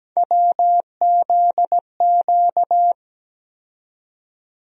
Morse code, heard as a keyed tone, about 17 wpm, 700 Hz. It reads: WZQ
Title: USA Call Sign Postfixes - 3 Letters 17wpm